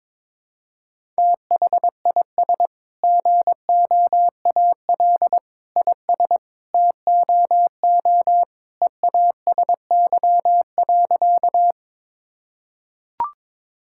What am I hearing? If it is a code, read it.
Morse code, heard as a keyed tone, about 21 wpm, 700 Hz. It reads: THIS GOAL IS TOO EASY.
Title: Sentences from Top 1000 Words - Review Part 5 - 22wpm